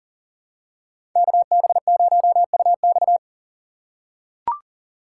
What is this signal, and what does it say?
Morse code, heard as a keyed tone, about 40 wpm, 700 Hz. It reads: KB0UX
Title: USA Call Signs - 5 Characters - 6x Character Spacing - 40wpm